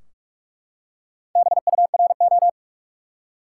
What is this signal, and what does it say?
Morse code, heard as a keyed tone, about 43 wpm, 700 Hz. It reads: BURO
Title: Top 180 QSO Elements 45wpm